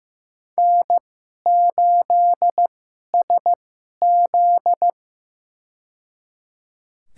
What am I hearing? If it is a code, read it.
Morse code, heard as a keyed tone, about 15 wpm, spaced as for 11 wpm, 690 Hz. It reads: N8SZ
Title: USA Call Signs - 4 Characters - 2x Character Spacing - 15wpm